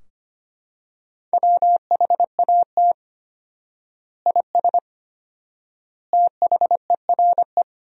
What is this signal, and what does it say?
Morse code, heard as a keyed tone, about 25 wpm, 700 Hz. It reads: WHAT IS THERE
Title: 4x Word-Spacing - Sentences from Top 100 Words - 25wpm